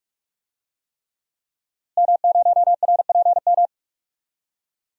Morse code, heard as a keyed tone, about 42 wpm, 700 Hz. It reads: M0RPM
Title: DXCC-100 Call Signs - 5 Characters - Repeated 3x - 45wpm